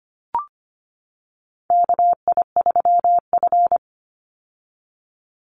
Morse code, heard as a keyed tone, about 25 wpm, 700 Hz. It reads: KI3F
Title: Call Signs - 25wpm